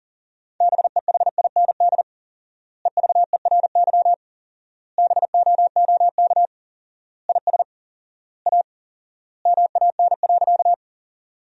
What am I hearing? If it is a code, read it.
Morse code, heard as a keyed tone, about 40 wpm, 700 Hz. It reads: BEHIND EVERY BOOK IS A MAN.
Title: 4x Word-Spacing - Sentences from Top 500 Words - 40wpm